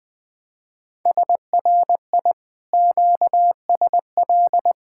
Morse code, heard as a keyed tone, about 21 wpm, 700 Hz. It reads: SRI QSL
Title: Speed-Racing Common QSO Phrases Encore - 20wpm